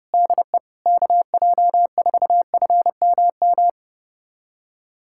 Morse code, heard as a keyed tone, about 30 wpm, 700 Hz. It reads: DE KJ4FMM
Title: USA Callsign DE Callsign - 6 Characters - 30wpm